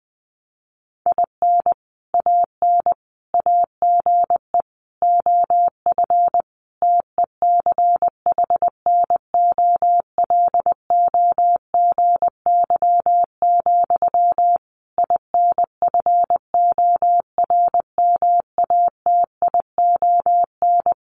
Morse code, heard as a keyed tone, about 20 wpm, 700 Hz. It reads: IN AN AGE OF TECHNOLOGY, INFORMATION
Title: Sentences from Top 1200 Words 20wpm